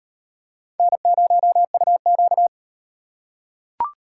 Morse code, heard as a keyed tone, about 38 wpm, 700 Hz. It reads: N0UQ